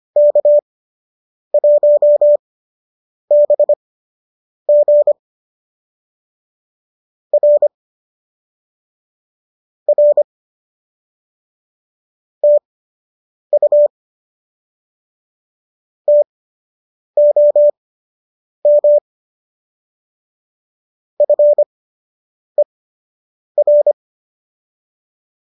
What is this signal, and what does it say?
Morse code, heard as a keyed tone, about 25 wpm, spaced as for 8 wpm, 570 Hz. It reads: K1BG R R TU TOM FER